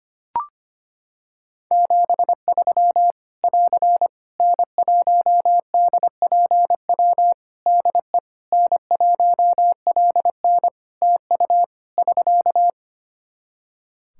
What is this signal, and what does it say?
Morse code, heard as a keyed tone, about 25 wpm, 700 Hz. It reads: 73 <AR> N1DPW DE N1LN TU <SK>
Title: Practice QSO - Exchange 4 - Ending - 25wpm